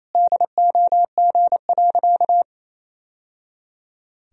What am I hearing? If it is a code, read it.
Morse code, heard as a keyed tone, about 28 wpm, 700 Hz. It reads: DOG.
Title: Sentences from Top 1000 Words - Review Part 2 - 28wpm